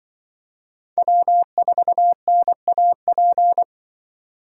Morse code, heard as a keyed tone, about 25 wpm, 700 Hz. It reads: W4NAP
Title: Speed Racing - USA 1x3 Call Signs 20wpm